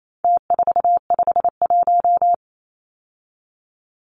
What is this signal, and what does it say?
Morse code, heard as a keyed tone, about 28 wpm, 700 Hz. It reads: T451
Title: Numbers - Contest Style - 4-Digit 28wpm